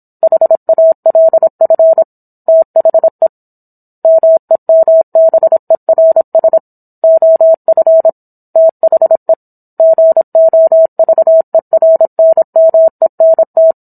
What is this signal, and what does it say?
Morse code, heard as a keyed tone, about 27 wpm, 650 Hz. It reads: HALF THE MEMBERS OF THE GOVERNMENT